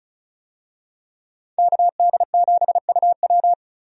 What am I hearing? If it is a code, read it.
Morse code, heard as a keyed tone, about 34 wpm, 700 Hz. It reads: KD7UW
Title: USA Call Signs - 5 Characters - 35wpm